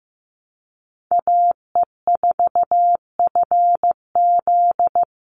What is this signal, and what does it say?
Morse code, heard as a keyed tone, about 15 wpm, 700 Hz. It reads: AE4FZ
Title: NA SOTA Chaser Callsigns 15wpm